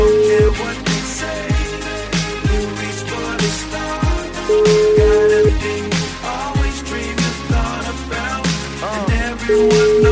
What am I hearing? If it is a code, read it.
Ringback tone (Europe, 425 Hz), with noise.